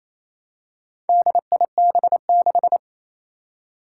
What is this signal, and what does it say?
Morse code, heard as a keyed tone, about 28 wpm, 700 Hz. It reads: DIB6
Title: Four Challenging Letter-Number Combinations 28wpm